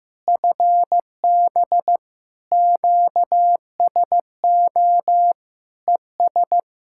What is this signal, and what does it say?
Morse code, heard as a keyed tone, about 15 wpm, 700 Hz. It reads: FB QSO ES
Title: Practice QSO - Exchange 4 - Ending - 15wpm